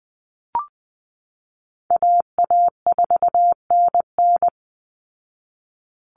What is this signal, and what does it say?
Morse code, heard as a keyed tone, about 20 wpm, 700 Hz. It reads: AA4NN
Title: Call Signs Repeated 3x - 20wpm